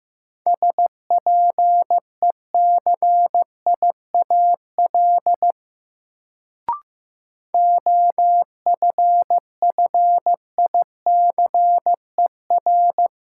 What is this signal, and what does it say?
Morse code, heard as a keyed tone, about 15 wpm, 700 Hz. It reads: SPECIAL OFFICER
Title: Sentences from Top 1200 Words - New Word Drill Alt - 15wpm